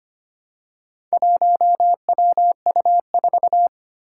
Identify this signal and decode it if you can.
Morse code, heard as a keyed tone, about 25 wpm, 700 Hz. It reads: 1WU4
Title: Four Challenging Letter-Number Combinations 25wpm